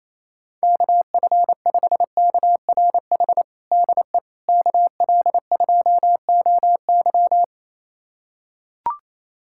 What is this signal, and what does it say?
Morse code, heard as a keyed tone, about 28 wpm, 700 Hz. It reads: KF5KRH DE KL2OY
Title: USA Callsign DE Callsign 28wpm